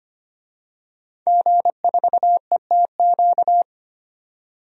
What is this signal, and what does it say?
Morse code, heard as a keyed tone, about 25 wpm, 700 Hz. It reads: G4ETQ
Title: DXCC-100 Call Signs - 5 Characters - 25wpm